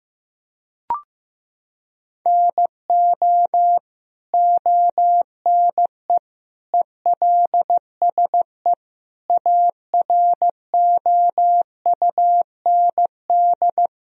Morse code, heard as a keyed tone, about 15 wpm, 700 Hz. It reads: NO ONE ELSE AROUND
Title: Sentences from Top 1000 Words - Review Part 3 - 15wpm